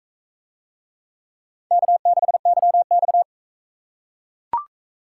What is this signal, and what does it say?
Morse code, heard as a keyed tone, about 39 wpm, 700 Hz. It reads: K6YX